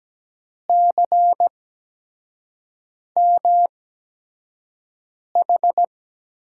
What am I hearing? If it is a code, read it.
Morse code, heard as a keyed tone, about 17 wpm, spaced as for 5 wpm, 700 Hz. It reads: CMH
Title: USA Call Signs - 5 Characters - 8x Character Spacing - 17wpm